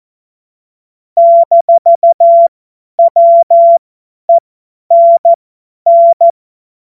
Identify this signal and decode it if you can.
Morse code, heard as a keyed tone, about 14 wpm, spaced as for 10 wpm, 680 Hz. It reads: -WENN
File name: Morsekurs / Witze / Witz 26. April 2022 50 F70